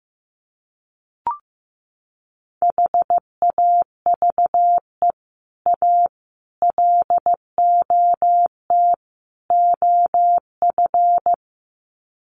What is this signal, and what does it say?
Morse code, heard as a keyed tone, about 15 wpm, 700 Hz. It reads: HAVE A LOT OF
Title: Sets of 4 Words 15wpm